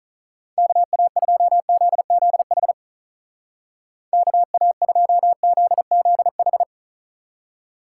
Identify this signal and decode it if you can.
Morse code, heard as a keyed tone, about 46 wpm, 700 Hz. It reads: KA2ZZH KA2ZZH